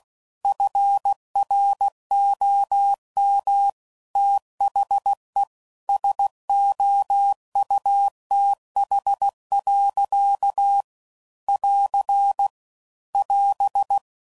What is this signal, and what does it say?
Morse code, heard as a keyed tone, about 16 wpm, 800 Hz. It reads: FROM THE SOUTH. <AR> <AS>
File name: CW Code Groups @15WPM/Weather Sets